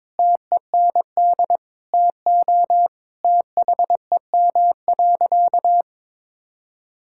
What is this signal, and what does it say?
Morse code, heard as a keyed tone, about 22 wpm, 700 Hz. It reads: TEND TO THEM.
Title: Sentences from Top 1000 Words - Review Part 5 - 22wpm